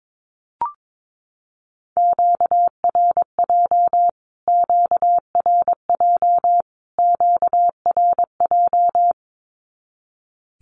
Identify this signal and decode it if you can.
Morse code, heard as a keyed tone, about 22 wpm, 700 Hz. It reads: QRJ QRJ QRJ
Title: Q-Codes Repeated 3x 22wpm